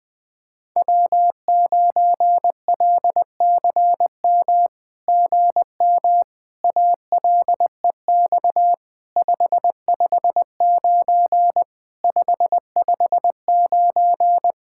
Morse code, heard as a keyed tone, about 20 wpm, 700 Hz. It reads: W9LCM GM ALEX 559 559
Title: SOTA QSO Practice 20wpm